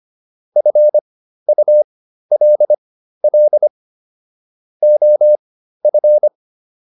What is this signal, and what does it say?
Morse code, heard as a keyed tone, about 25 wpm, spaced as for 13 wpm, 600 Hz. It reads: FULL OF